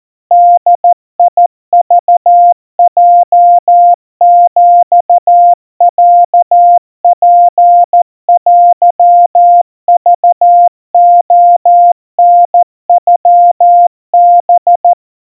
Morse code (Japanese Wabun code), heard as a keyed tone, about 14 wpm, 700 Hz. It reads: ボクヲヒロツテクレタノハ
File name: kawa / Morse sample / kamyu 50 cpm 700 Hz Crispy